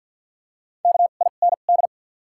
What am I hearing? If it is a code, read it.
Morse code, heard as a keyed tone, about 53 wpm, spaced as for 37 wpm, 700 Hz. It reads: KIND
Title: 2x Character Spacing - Top 200 Words - 50wpm